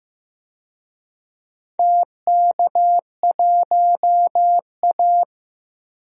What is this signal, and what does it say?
Morse code, heard as a keyed tone, about 16 wpm, 700 Hz. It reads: TK1A
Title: DXCC-100 Call Signs - 4 Characters - 15wpm